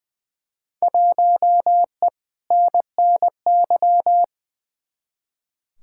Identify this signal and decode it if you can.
Morse code, heard as a keyed tone, about 20 wpm, 700 Hz. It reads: 1E NNY
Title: Field Day Exchanges - 20wpm